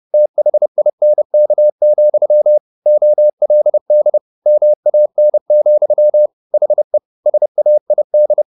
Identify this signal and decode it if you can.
Morse code, heard as a keyed tone, about 30 wpm, 600 Hz. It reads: THINK, OLD MAN, HE SAID